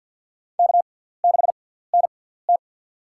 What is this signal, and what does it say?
Morse code, heard as a keyed tone, about 52 wpm, spaced as for 17 wpm, 700 Hz. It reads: K6NT
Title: USA Call Signs - 4 Characters - 6x Character Spacing - 50wpm